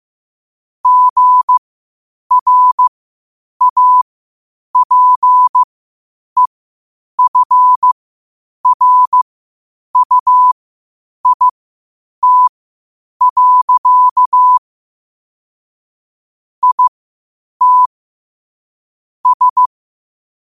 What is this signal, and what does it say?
Morse code, heard as a keyed tone, about 15 wpm, spaced as for 8 wpm, 1000 Hz. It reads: GRAPEFRUIT. IT S
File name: qotdfiles / 8wpm